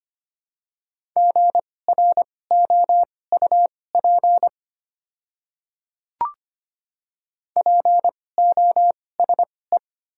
Morse code, heard as a keyed tone, about 25 wpm, spaced as for 18 wpm, 700 Hz. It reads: GROUP POSE
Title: 2x Character Spacing - Top 500 Words - 25wpm